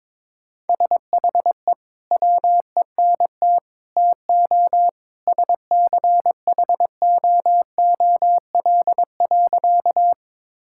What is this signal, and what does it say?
Morse code, heard as a keyed tone, about 21 wpm, 700 Hz. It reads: SHE WENT TO SCHOOL.
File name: practice / 22wpm / Sentences with 2-Word Phrases - 22wpm